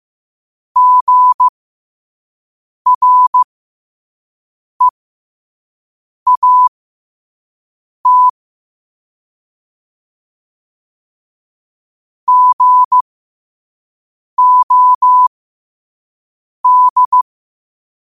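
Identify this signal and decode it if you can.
Morse code, heard as a keyed tone, about 15 wpm, spaced as for 5 wpm, 1000 Hz. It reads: GREAT GOD
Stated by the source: Morse code sent at 5 wpm.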